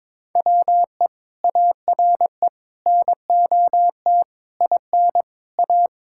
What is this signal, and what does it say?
Morse code, heard as a keyed tone, about 22 wpm, 700 Hz. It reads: WE ARE NOT IN A
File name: practice / 22wpm / Sets of 7 Words - 22wpm